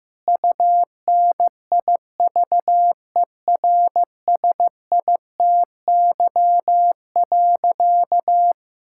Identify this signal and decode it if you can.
Morse code, heard as a keyed tone, about 15 wpm, 700 Hz. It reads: UNIVERSITY.
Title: Sentences from Top 1000 Words - Review Part 5 - 15wpm